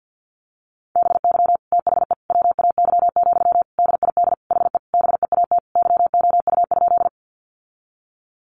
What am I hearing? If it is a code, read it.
Morse code, heard as a keyed tone, about 50 wpm, 700 Hz. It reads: BY THE WAY, DID HE BEAT YOU?
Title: Sentences from Top 1300 Words 50wpm